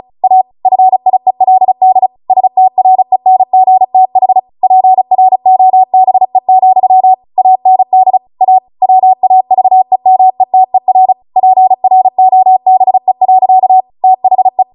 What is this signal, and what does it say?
Morse code, heard as a keyed tone, about 36 wpm, 750 Hz. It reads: A FIELD STRENGTH PROBE, AND A WAVEMETER PROBE. THE